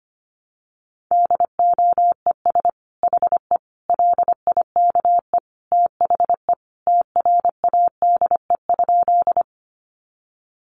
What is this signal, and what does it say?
Morse code, heard as a keyed tone, about 25 wpm, 700 Hz. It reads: DOES HE LIKE THE TRADE?
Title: Sentences from Top 600 Words 25wpm